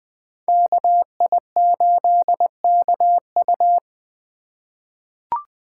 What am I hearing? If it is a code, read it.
Morse code, heard as a keyed tone, about 20 wpm, 700 Hz. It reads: KI8KU
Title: USA Call Signs - 5 Characters - 4x Character Spacing - 20wpm